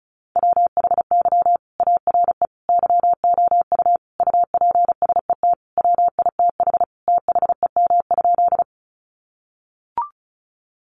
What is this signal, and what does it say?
Morse code, heard as a keyed tone, about 34 wpm, 700 Hz. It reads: WHY ARE YOU UPSET WITH THEM?
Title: Sentences from Top 800 Words 35wpm